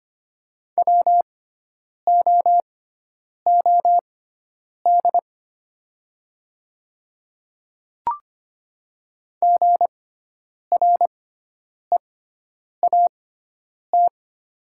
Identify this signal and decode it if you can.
Morse code, heard as a keyed tone, about 25 wpm, spaced as for 9 wpm, 700 Hz. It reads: WOOD GREAT
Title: 6x Character Spacing - Top 500 Words - 25wpm